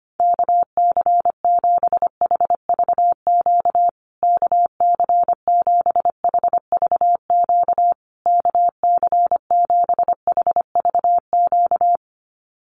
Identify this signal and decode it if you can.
Morse code, heard as a keyed tone, about 25 wpm, 700 Hz. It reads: KC7HVQ KC7HVQ KC7HVQ
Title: USA Call Signs Repeated 3x 25wpm